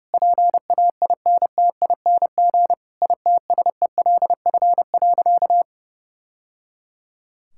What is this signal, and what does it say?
Morse code, heard as a keyed tone, about 30 wpm, 700 Hz. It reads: PAINTING ITSELF.
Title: Sentences from Top 1000 Words - Review Part 6 - 30wpm